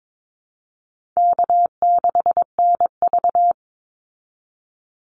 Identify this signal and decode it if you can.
Morse code, heard as a keyed tone, about 22 wpm, 700 Hz. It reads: K6NV
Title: Easy Call Signs Repeated 3x - 22wpm